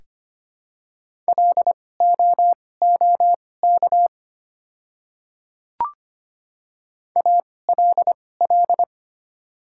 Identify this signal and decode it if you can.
Morse code, heard as a keyed tone, about 25 wpm, spaced as for 18 wpm, 700 Hz. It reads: LOOK ALL
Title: 2x Character Spacing - Top 200 Words - 25wpm